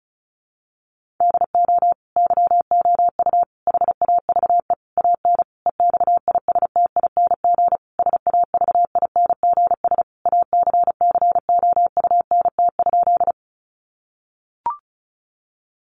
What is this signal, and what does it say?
Morse code, heard as a keyed tone, about 34 wpm, 700 Hz. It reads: DO YOU HAVE AN EXISTING SAVINGS ACCOUNT?
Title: Sentences from Top 1500 Words 35wpm